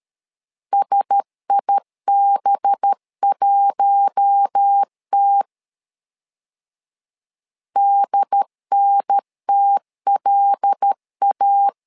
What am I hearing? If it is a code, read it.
Morse code, heard as a keyed tone, about 13 wpm, 800 Hz. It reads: SIB1T DNTLA
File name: LEZIONI / 5 LBVC / 5 LB 12wpm 1906